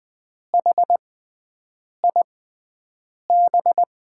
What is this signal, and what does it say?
Morse code, heard as a keyed tone, about 20 wpm, spaced as for 7 wpm, 700 Hz. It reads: HIB
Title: USA Call Signs - 6 Characters - 6x Character Spacing - 20wpm